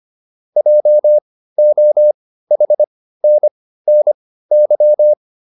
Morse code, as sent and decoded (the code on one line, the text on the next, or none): .--- --- .... -. -. -.--
JOHNNY